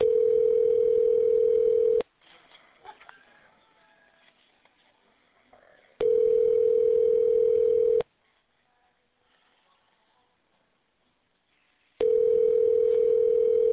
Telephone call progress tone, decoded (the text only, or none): ringback tone